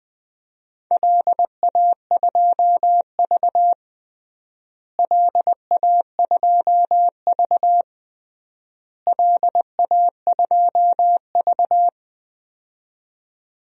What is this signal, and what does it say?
Morse code, heard as a keyed tone, about 20 wpm, 700 Hz. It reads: LA2V LA2V LA2V
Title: DXCC-100 Call Signs - 4 Characters - Repeated 3x - 20wpm